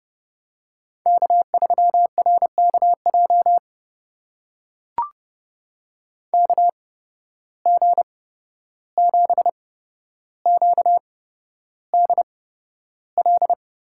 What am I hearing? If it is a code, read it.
Morse code, heard as a keyed tone, about 31 wpm, spaced as for 12 wpm, 700 Hz. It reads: K3RKJ KG7QDL
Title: USA Call Signs - 8x Character Spacing - 30wpm